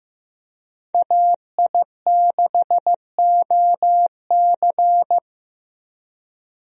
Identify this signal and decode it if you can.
Morse code, heard as a keyed tone, about 15 wpm, 700 Hz. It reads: AI6OC
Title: USA Call Signs - 5 Characters - Repeated 2x - 15wpm